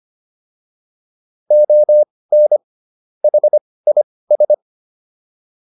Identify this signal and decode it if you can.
Morse code, heard as a keyed tone, about 25 wpm, spaced as for 18 wpm, 600 Hz. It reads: ON HIS
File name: WD104 18